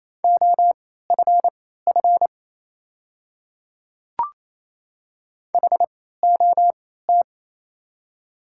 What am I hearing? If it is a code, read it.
Morse code, heard as a keyed tone, about 27 wpm, spaced as for 16 wpm, 690 Hz. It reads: OFF HOT
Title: 3x Character Spacing - Top 300 Words - 28wpm